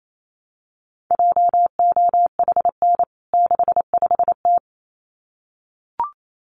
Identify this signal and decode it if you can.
Morse code, heard as a keyed tone, about 28 wpm, 700 Hz. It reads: JOHN 65T
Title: CWT Exchange 28wpm